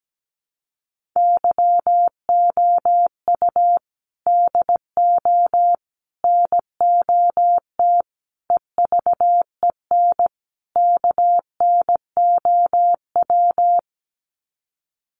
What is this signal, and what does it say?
Morse code, heard as a keyed tone, about 17 wpm, 700 Hz. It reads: YOU DO NOT EVEN KNOW
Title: Sets of 5 Words Encore2 17wpm